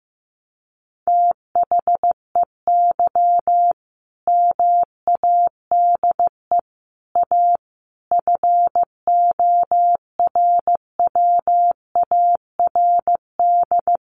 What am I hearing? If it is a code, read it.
Morse code, heard as a keyed tone, about 15 wpm, 700 Hz. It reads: THEY MADE A FORWARD
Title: Sentences from Top 1300 Words 15wpm